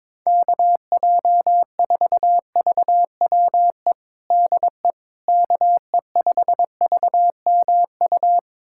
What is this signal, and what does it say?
Morse code, heard as a keyed tone, about 22 wpm, 700 Hz. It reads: KJ4VWE DE KE5VMU
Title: USA Callsign DE Callsign - 6 Characters - 22wpm